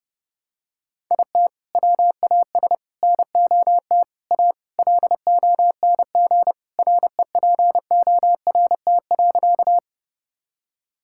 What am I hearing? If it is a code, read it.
Morse code, heard as a keyed tone, about 30 wpm, 700 Hz. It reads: IT WAS NOT A LONG REPORT.
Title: Sentences from Top 1000 Words - Review Part 6 - 30wpm